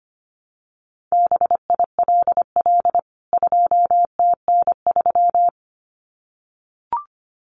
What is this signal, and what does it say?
Morse code, heard as a keyed tone, about 25 wpm, 700 Hz. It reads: BILL 2TN3
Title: CWT Exchange 25wpm